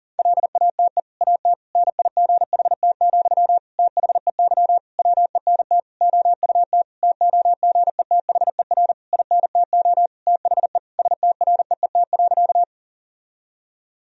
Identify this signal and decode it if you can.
Morse code, heard as a keyed tone, about 37 wpm, 700 Hz. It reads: LATE AT NIGHT, THEY WENT OUT TOGETHER INTO THE STREET.
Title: Sentences from Top 1000 Words - Review Part 3 - 40wpm